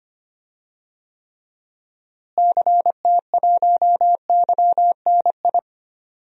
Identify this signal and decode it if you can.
Morse code, heard as a keyed tone, about 25 wpm, 700 Hz. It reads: CT1YNI
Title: DXCC-100 Call Signs - 6 Characters - Repeated 2x - 25wpm